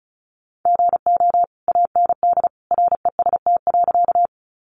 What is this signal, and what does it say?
Morse code, heard as a keyed tone, about 34 wpm, 700 Hz. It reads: GO AND REST.
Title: Sentences from Top 500 Words 35wpm